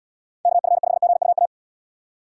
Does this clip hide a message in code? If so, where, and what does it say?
Yes: Morse code from 0.5 to 1.5 s, heard as a keyed tone, about 79 wpm, 700 Hz. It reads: KF5GUA